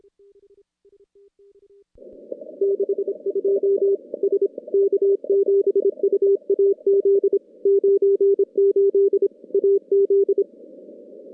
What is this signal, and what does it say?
Morse code, heard as a keyed tone, about 26 wpm, 410 Hz. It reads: 62 S K7UAZ 98 AZ
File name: contests / CW-SS-k5tr- -2300